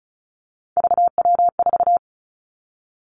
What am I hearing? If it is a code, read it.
Morse code, heard as a keyed tone, about 36 wpm, 700 Hz. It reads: VW4
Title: Three Challenging Letter-Number Combinations 35wpm